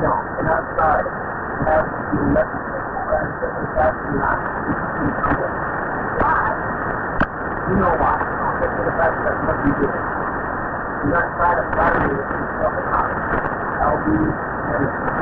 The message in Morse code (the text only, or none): none